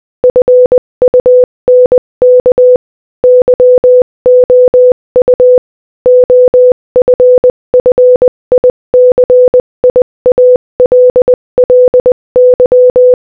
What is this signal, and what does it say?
Morse code, heard as a keyed tone, about 20 wpm, 500 Hz. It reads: FUNK YOU OFFICIALLY
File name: Morse-Code-Audio-Speed-20-4